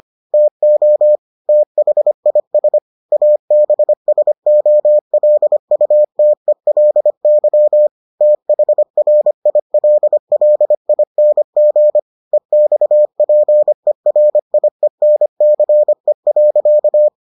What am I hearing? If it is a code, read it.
Morse code, heard as a keyed tone, about 25 wpm, 600 Hz. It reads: TO THIS ABSOLUTELY THRILLING EXPERIENCE.